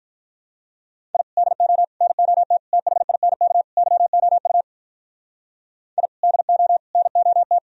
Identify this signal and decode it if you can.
Morse code, heard as a keyed tone, about 46 wpm, 700 Hz. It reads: I DO NOT THINK YOU I DO NOT